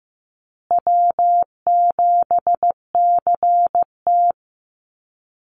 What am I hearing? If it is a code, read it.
Morse code, heard as a keyed tone, about 14 wpm, 700 Hz. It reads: W7CT
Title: Easy Call Signs Repeated 3x - 15wpm